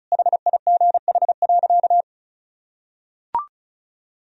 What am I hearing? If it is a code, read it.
Morse code, heard as a keyed tone, about 34 wpm, 700 Hz. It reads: HIGH.
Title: Sentences from Top 1000 Words - Review Part 1 - 35wpm